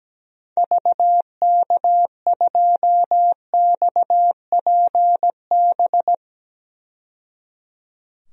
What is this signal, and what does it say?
Morse code, heard as a keyed tone, about 17 wpm, 700 Hz. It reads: VK2XPB
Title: DXCC-100 Call Signs - 6 Characters - 17wpm